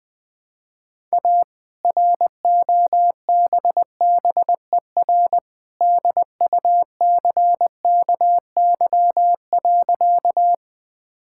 Morse code, heard as a keyed tone, about 20 wpm, 700 Hz. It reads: A ROBBER DUCKY.
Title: All-Age Appropriate Jokes - 20wpm